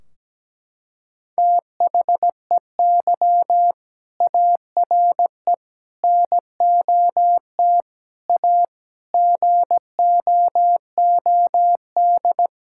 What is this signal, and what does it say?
Morse code, heard as a keyed tone, about 17 wpm, 700 Hz. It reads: THEY ARE NOT A GOOD
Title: Sentences from Top 1600 Words 17wpm